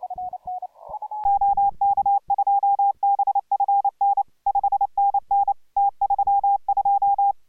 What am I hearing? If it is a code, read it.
Morse code, heard as a keyed tone, about 29 wpm, 800 Hz. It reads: OK2BFN 5NN T32